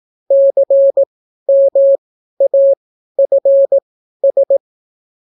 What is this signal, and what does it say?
Morse code, heard as a keyed tone, about 18 wpm, spaced as for 12 wpm, 550 Hz. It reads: CMAFS